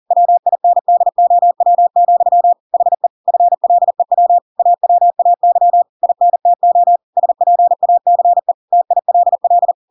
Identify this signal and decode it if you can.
Morse code, heard as a keyed tone, about 40 wpm, 700 Hz. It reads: WINDOW, HE FLEW AWAY INTO SPACE TILL
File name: Pinocchio / Pinocchio ch5 40 40 ik1hge